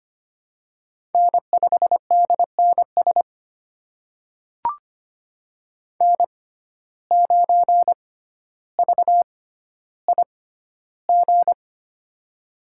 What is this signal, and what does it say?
Morse code, heard as a keyed tone, about 25 wpm, spaced as for 13 wpm, 700 Hz. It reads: N5DNS N9VIG
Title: USA Call Signs - 5 Characters - 6x Character Spacing - 25wpm